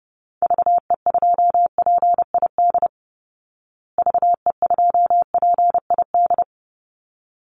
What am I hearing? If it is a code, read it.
Morse code, heard as a keyed tone, about 30 wpm, 700 Hz. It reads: VE2PID VE2PID
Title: Call Signs Repeated 3x - 30wpm